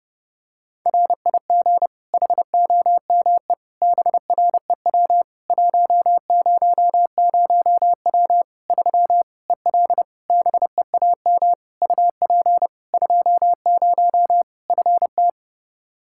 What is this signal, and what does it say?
Morse code, heard as a keyed tone, about 30 wpm, 700 Hz. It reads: RIG HOME BREW 100W 3 EL BEAM UP 20 FT
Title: Practice QSO - Exchange 2 - Info - 30wpm